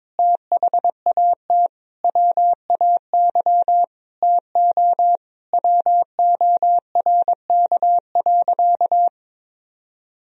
Morse code, heard as a keyed tone, about 21 wpm, 700 Hz. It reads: THAT WAY TO WORK.
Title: Sentences from Top 1000 Words - Review Part 3 - 22wpm